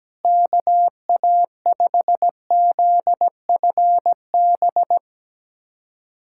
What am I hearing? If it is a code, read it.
Morse code, heard as a keyed tone, about 17 wpm, 700 Hz. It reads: KA5ZFB
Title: USA Callsign DE Callsign - 6 Characters - 17wpm